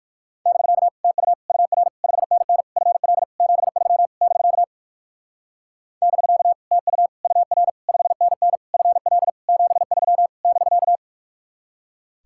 Morse code, heard as a keyed tone, about 59 wpm, 700 Hz. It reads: <BK> TU UR 5NN FL 73 <BK> <BK> TU UR 5NN FL 73 <BK>